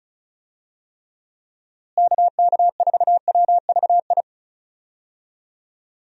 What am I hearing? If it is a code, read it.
Morse code, heard as a keyed tone, about 34 wpm, 700 Hz. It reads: KK4WVI
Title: USA Call Signs - 6 Characters - Repeated 3x - 35wpm